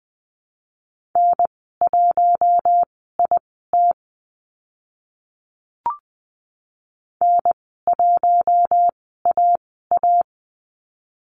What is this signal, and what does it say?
Morse code, heard as a keyed tone, about 20 wpm, spaced as for 14 wpm, 700 Hz. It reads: N1IT N1AA
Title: 062 Lesson 11g - Letter h - Call Signs 20wpm